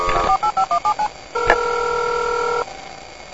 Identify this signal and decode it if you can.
DTMF, heard as telephone touch tones, tones of about 90 ms. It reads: 48517